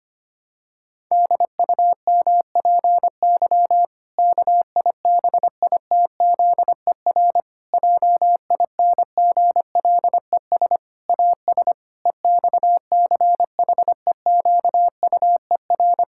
Morse code, heard as a keyed tone, about 25 wpm, 700 Hz. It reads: DUMPY KIBITZER JINGLES AS EXCHEQUER
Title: Pangram Challenge v2 - 25wpm